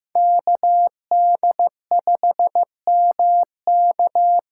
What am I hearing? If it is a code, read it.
Morse code, heard as a keyed tone, about 15 wpm, 700 Hz. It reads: KD5MK
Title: USA Call Signs - 6 Characters - 8x Character Spacing - 15wpm